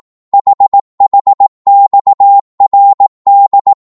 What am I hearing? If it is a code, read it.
Morse code, heard as a keyed tone, about 18 wpm, 800 Hz. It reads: HHXRD
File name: Morse / ltr-18wpm